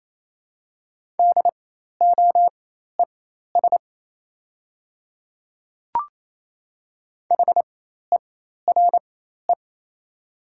Morse code, heard as a keyed tone, about 28 wpm, spaced as for 13 wpm, 700 Hz. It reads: DOES HERE
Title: 4x Character Spacing - Top 500 Words - 28wpm